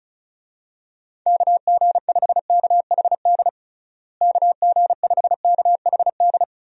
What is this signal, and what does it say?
Morse code, heard as a keyed tone, about 36 wpm, 700 Hz. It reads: KG5KHD KG5KHD
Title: USA Call Signs - 6 Characters - Repeated 3x - 35wpm